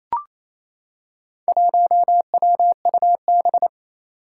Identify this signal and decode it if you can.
Morse code, heard as a keyed tone, about 27 wpm, 700 Hz. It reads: 1WUB
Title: Four Challenging Letter-Number Combinations 28wpm